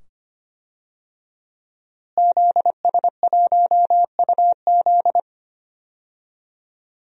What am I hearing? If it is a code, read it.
Morse code, heard as a keyed tone, about 25 wpm, 700 Hz. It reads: ZS1UZ
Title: DXCC-100 Call Signs - 5 Characters - Repeated 2x - 25wpm